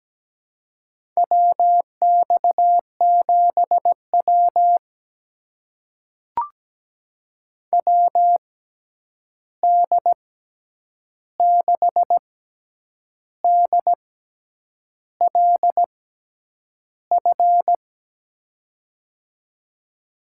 Morse code, heard as a keyed tone, about 17 wpm, spaced as for 8 wpm, 700 Hz. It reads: WX7W WD6DLF
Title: USA Call Signs - 6x Character Spacing - 17wpm